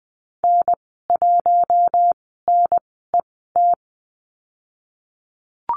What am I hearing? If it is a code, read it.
Morse code, heard as a keyed tone, about 20 wpm, spaced as for 15 wpm, 700 Hz. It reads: N1NET
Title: 048 Lesson 9g - Number 4 - Call Signs 20wpm